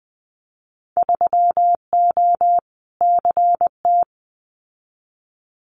Morse code, heard as a keyed tone, about 20 wpm, 700 Hz. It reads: 3O CT
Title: Winter Field Day Exchanges Encore 20wpm